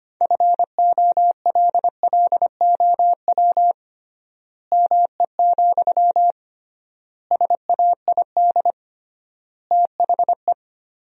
Morse code, heard as a keyed tone, about 25 wpm, 700 Hz. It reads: FOLLOW ME, SAID THE
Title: 3x Word-Spacing - Sentences from Top 200 Words - 25wpm